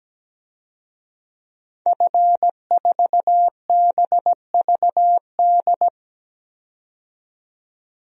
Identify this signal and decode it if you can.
Morse code, heard as a keyed tone, about 17 wpm, 700 Hz. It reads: F4BVD
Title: DXCC-100 Call Signs - 5 Characters - 17wpm